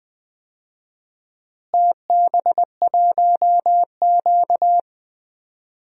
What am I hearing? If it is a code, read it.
Morse code, heard as a keyed tone, about 20 wpm, 700 Hz. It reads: TB1Q
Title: DXCC-100 Call Signs - 4 Characters - Repeated 3x - 20wpm